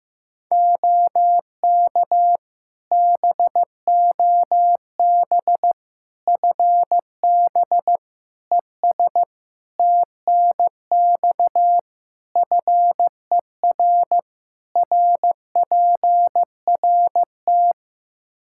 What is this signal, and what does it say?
Morse code, heard as a keyed tone, about 15 wpm, 700 Hz. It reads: OK BOB FB ES TNX FER RPRT
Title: Practice QSO - Exchange 2 - Info - 15wpm